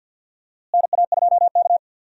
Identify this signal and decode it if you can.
Morse code, heard as a keyed tone, about 51 wpm, 700 Hz. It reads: NA2K
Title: USA Call Signs - 4 Characters - Repeated 2x - 50wpm